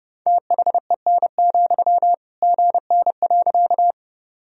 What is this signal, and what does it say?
Morse code, heard as a keyed tone, about 30 wpm, 700 Hz. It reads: THEN, GN.
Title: Sentences from Top 300 Words and Abbreviations 30wpm